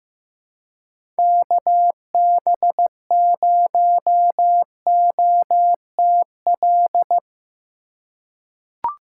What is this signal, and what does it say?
Morse code, heard as a keyed tone, about 15 wpm, 700 Hz. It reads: KB0OTL
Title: USA Call Signs - 6 Characters - 3x Character Spacing - 15wpm